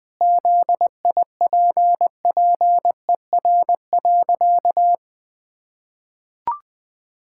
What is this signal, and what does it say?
Morse code, heard as a keyed tone, about 20 wpm, 700 Hz. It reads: ZIPPER.
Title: Pangram Challenge v2 - 20wpm